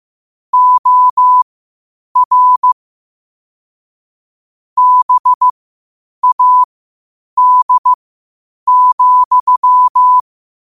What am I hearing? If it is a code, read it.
Morse code, heard as a keyed tone, about 15 wpm, spaced as for 8 wpm, 1000 Hz. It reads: OR BAD,
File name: qotdfiles / 8wpm